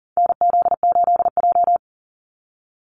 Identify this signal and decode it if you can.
Morse code, heard as a keyed tone, about 40 wpm, 700 Hz. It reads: NZ8J
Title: Call Signs Repeated 2x - 40wpm